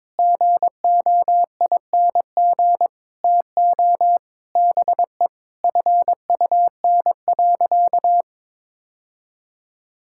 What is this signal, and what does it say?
Morse code, heard as a keyed tone, about 22 wpm, 700 Hz. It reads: GOING TO BE FUN.
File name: practice / 22wpm / Sentences with 4-Word Phrases - 22wpm